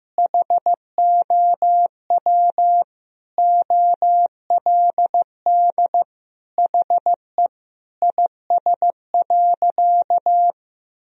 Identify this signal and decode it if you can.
Morse code, heard as a keyed tone, about 15 wpm, 700 Hz. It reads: HOW OLD HE IS.
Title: Sentences from Top 1000 Words - Review Part 4 - 15wpm